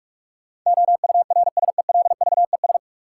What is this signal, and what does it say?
Morse code, heard as a keyed tone, about 47 wpm, 700 Hz. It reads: OURSELVES
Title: Sentences from Top 1600 Words - New Word Drill - 45wpm